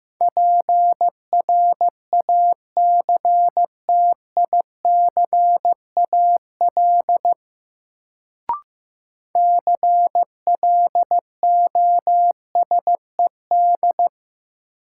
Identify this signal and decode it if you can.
Morse code, heard as a keyed tone, about 15 wpm, 700 Hz. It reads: PRACTICAL CLOSED
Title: Sentences from Top 1400 Words - New Word Drill Alt - 15wpm